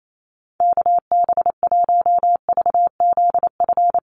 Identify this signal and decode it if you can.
Morse code, heard as a keyed tone, about 28 wpm, 700 Hz. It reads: KB1VZF
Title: USA Call Signs Repeated 2x 28wpm